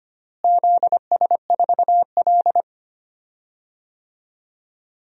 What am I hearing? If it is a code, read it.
Morse code, heard as a keyed tone, about 25 wpm, 700 Hz. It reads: ZS4L
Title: DXCC-100 Call Signs - 4 Characters - Repeated 2x - 25wpm